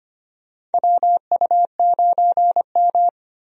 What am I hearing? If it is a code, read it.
Morse code, heard as a keyed tone, about 25 wpm, 700 Hz. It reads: WU9M
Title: USA Call Signs - 4 Characters - Repeated 2x - 25wpm